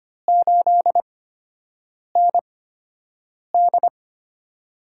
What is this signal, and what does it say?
Morse code, heard as a keyed tone, about 25 wpm, spaced as for 7 wpm, 700 Hz. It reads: 8ND
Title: USA Call Signs - 8x Character Spacing - 25wpm